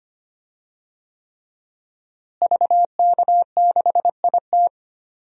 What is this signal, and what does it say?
Morse code, heard as a keyed tone, about 25 wpm, 700 Hz. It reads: VK6IT
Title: DXCC-100 Call Signs - 5 Characters - Repeated 3x - 25wpm